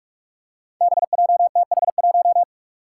Morse code, heard as a keyed tone, about 46 wpm, 700 Hz. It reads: BJTH1
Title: Five Challenging Letter-Number Combinations 45wpm